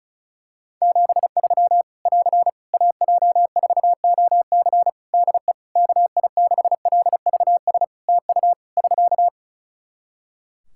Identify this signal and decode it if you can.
Morse code, heard as a keyed tone, about 34 wpm, 700 Hz. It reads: 73 <AR> AJ4OC DE KI6LVS TU <SK>
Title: Practice QSO - Exchange 4 - Ending - 35wpm